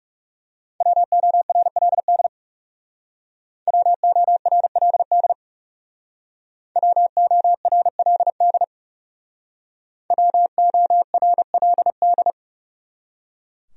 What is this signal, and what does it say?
Morse code, heard as a keyed tone, about 39 wpm, 700 Hz. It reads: WORLD WORLD WORLD WORLD